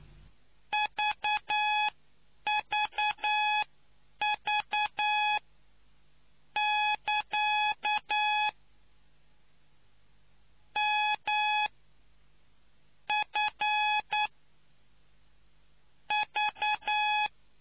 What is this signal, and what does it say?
Morse code, heard as a keyed tone, about 9 wpm, spaced as for 5 wpm, 840 Hz. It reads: VVV<KA>MFV